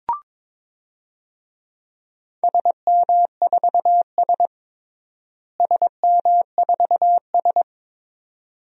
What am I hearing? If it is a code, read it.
Morse code, heard as a keyed tone, about 22 wpm, 700 Hz. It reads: SM4S SM4S
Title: DXCC-100 Call Signs - 4 Characters - Repeated 3x - 22wpm